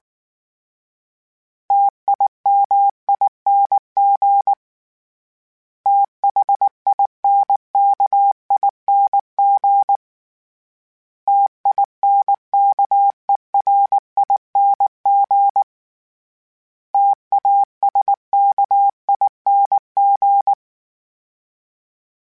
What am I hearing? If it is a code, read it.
Morse code, heard as a keyed tone, about 19 wpm, 800 Hz. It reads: TIMING THINKING TINKERING TASKING
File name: Special INGs / Special INGs-mp3s / Sngl Ing @18